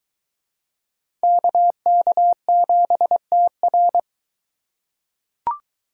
Morse code, heard as a keyed tone, about 21 wpm, 700 Hz. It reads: KK7TR